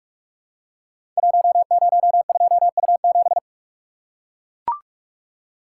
Morse code, heard as a keyed tone, about 44 wpm, 700 Hz. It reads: 102U7
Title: Code Groups - 5 Letter:Numbers - 45wpm